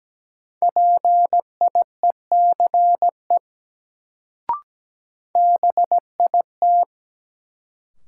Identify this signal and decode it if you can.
Morse code, heard as a keyed tone, about 17 wpm, 700 Hz. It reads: PIECE BIT
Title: Sets of 3 Words - New Word Drill - 17wpm